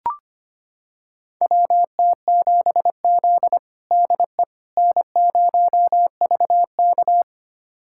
Morse code, heard as a keyed tone, about 25 wpm, 700 Hz. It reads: WT7Z DE N0VK
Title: USA Callsign DE Callsign - 4 Characters - 25wpm